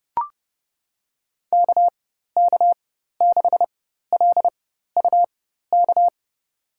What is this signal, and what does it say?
Morse code, heard as a keyed tone, about 30 wpm, spaced as for 14 wpm, 700 Hz. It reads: KK6LUK
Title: USA Call Signs - 6 Characters - 4x Character Spacing - 30wpm